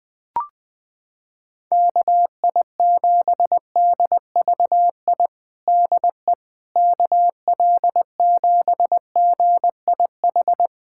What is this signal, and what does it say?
Morse code, heard as a keyed tone, about 21 wpm, 700 Hz. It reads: KI7DVI DE KL7GIH
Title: USA Callsign DE Callsign - 6 Characters - 20wpm